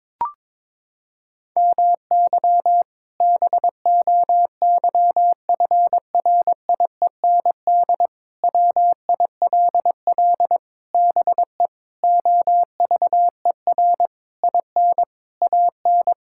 Morse code, heard as a keyed tone, about 22 wpm, 700 Hz. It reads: MY BOYFRIEND WILL BE OVER IN AN
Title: Sentences from Top 1000 Words - Review Part 5 - 22wpm